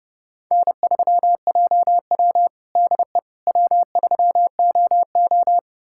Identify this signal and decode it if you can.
Morse code, heard as a keyed tone, about 32 wpm, 700 Hz. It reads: N3JW DE W3OO
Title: USA Callsign DE Callsign - 4 Characters - 30wpm